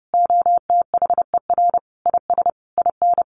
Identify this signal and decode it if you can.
Morse code, heard as a keyed tone, about 30 wpm, 700 Hz. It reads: OTHER IS IN
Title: Sentences from Top 1500 Words 30wpm